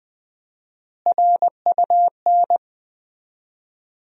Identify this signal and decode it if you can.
Morse code, heard as a keyed tone, about 19 wpm, 700 Hz. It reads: RUN
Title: 4x Word-Spacing - Sentences from Top 300 Words - 20wpm